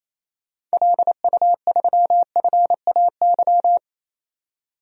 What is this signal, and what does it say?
Morse code, heard as a keyed tone, about 28 wpm, 700 Hz. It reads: LU3FAY
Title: DXCC-100 Call Signs - 6 Characters - 28wpm